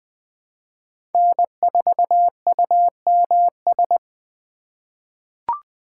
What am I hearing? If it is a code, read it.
Morse code, heard as a keyed tone, about 20 wpm, 700 Hz. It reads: N4UMS
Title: USA Call Signs - 5 Characters - 6x Character Spacing - 20wpm